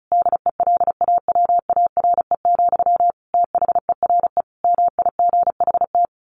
Morse code, heard as a keyed tone, about 34 wpm, 700 Hz. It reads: DELAWARE, THERE MIGHT
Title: A Sound Of Thunder 35wpm